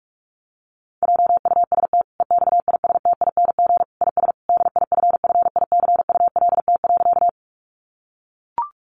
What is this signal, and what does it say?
Morse code, heard as a keyed tone, about 44 wpm, 700 Hz. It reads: JUST EXISTING IS DIFFICULT.
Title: Sentences from Top 1500 Words 45wpm